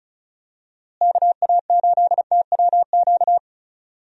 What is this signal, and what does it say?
Morse code, heard as a keyed tone, about 36 wpm, 700 Hz. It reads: KA8TWQ
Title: USA Call Signs - 6 Characters - Repeated 3x - 35wpm